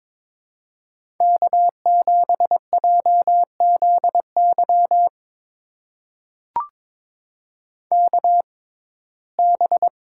Morse code, heard as a keyed tone, about 23 wpm, spaced as for 16 wpm, 700 Hz. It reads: K7JZY KB
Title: USA Call Signs - 5 Characters - 6x Character Spacing - 22wpm